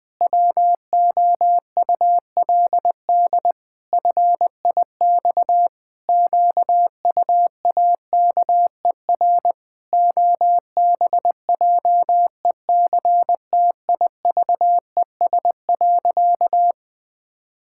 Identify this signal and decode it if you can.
Morse code, heard as a keyed tone, about 20 wpm, 700 Hz. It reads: WOULD FIX QUAKER OBJECTIVES.
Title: Pangram Challenge v2 - 20wpm